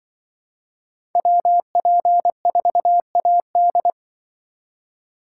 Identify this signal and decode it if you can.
Morse code, heard as a keyed tone, about 24 wpm, 700 Hz. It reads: WP4AD
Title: Speed Racing - USA 2x2 Call Signs 20wpm